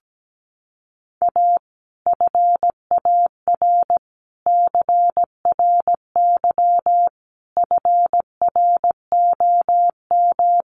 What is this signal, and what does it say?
Morse code, heard as a keyed tone, about 17 wpm, 700 Hz. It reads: A FAR CRY FROM
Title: Sets of 5 Words Encore 17wpm